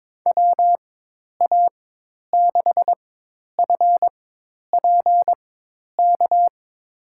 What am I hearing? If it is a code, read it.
Morse code, heard as a keyed tone, about 22 wpm, spaced as for 10 wpm, 700 Hz. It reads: WA6FPK
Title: USA Call Signs - 6 Characters - 4x Character Spacing - 22wpm